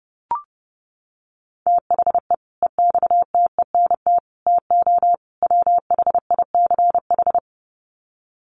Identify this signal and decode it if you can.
Morse code, heard as a keyed tone, about 30 wpm, 700 Hz. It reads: THE EXTENT TO WHICH
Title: Sets of 4 Words 30wpm